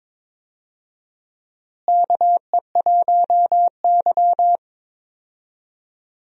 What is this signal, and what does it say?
Morse code, heard as a keyed tone, about 22 wpm, 700 Hz. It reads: KE1Y
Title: USA Call Signs - 4 Characters - Repeated 2x - 22wpm